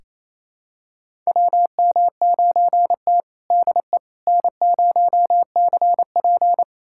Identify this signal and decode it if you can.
Morse code, heard as a keyed tone, about 28 wpm, 700 Hz. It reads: WM9T DE N0CP
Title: USA Callsign DE Callsign - 4 Characters - 28wpm